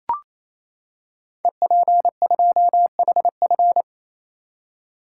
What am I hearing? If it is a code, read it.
Morse code, heard as a keyed tone, about 29 wpm, 700 Hz. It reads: EP2HF
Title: Code Groups - 5 Letter:Numbers - 28wpm